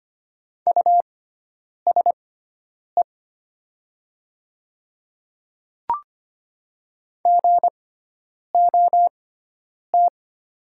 Morse code, heard as a keyed tone, about 25 wpm, spaced as for 9 wpm, 700 Hz. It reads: USE GOT
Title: 6x Character Spacing - Top 500 Words - 25wpm